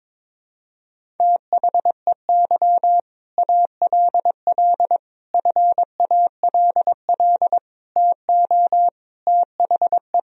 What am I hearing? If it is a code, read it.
Morse code, heard as a keyed tone, about 22 wpm, 700 Hz. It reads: THEY ALL FALL TO THE
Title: Sentences from Top 500 Words and Abbreviations 22wpm